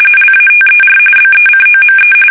RTTY, about 45.45 baud. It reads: LATER BECOME
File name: fsk850 / battleships